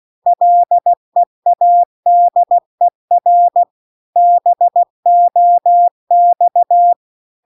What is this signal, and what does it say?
Morse code, heard as a keyed tone, about 16 wpm, 700 Hz. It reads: LEADER BOX
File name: CW-The-Big-Twoheaded-River-2-16WPM